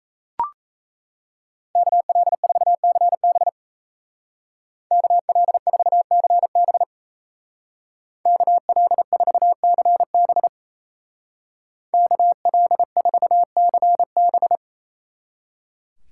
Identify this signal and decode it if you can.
Morse code, heard as a keyed tone, about 35 wpm, 700 Hz. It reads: KL4CB KL4CB KL4CB KL4CB
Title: Speed Racing - USA Call Signs 28wpm